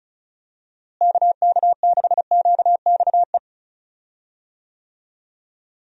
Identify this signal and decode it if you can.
Morse code, heard as a keyed tone, about 36 wpm, 700 Hz. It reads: KK6QXE
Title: USA Call Signs - 6 Characters - Repeated 2x - 35wpm